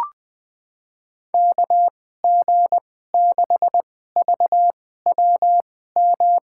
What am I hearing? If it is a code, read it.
Morse code, heard as a keyed tone, about 20 wpm, spaced as for 14 wpm, 700 Hz. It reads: KG6VWM
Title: USA Call Signs - 6 Characters - 2x Character Spacing - 20wpm